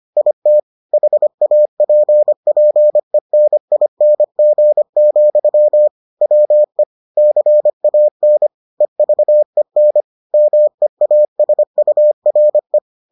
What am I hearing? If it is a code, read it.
Morse code, heard as a keyed tone, about 25 wpm, 600 Hz. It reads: IT HAPPENING, WE CAN EVEN MEASURE